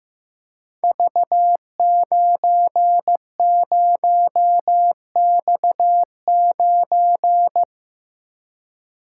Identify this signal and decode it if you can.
Morse code, heard as a keyed tone, about 15 wpm, 700 Hz. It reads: V90X9
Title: Code Groups - 5 Letter:Numbers - 15wpm